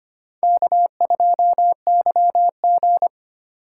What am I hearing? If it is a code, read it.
Morse code, heard as a keyed tone, about 25 wpm, 700 Hz. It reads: K2YG
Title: USA Call Signs - 4 Characters - 2x Character Spacing - 25wpm